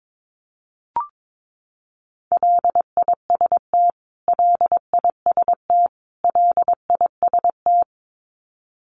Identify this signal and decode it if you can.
Morse code, heard as a keyed tone, about 22 wpm, 700 Hz. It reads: LIST LIST LIST
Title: Top 500 Words Repeated 22wpm